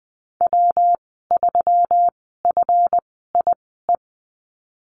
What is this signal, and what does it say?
Morse code, heard as a keyed tone, about 20 wpm, spaced as for 14 wpm, 700 Hz. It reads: W3FIE
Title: 161 Lesson 26g - Letter g - Call Signs 20wpm